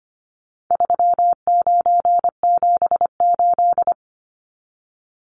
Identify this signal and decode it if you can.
Morse code, heard as a keyed tone, about 25 wpm, 700 Hz. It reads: 3978
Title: Numbers - 4-Digit 25wpm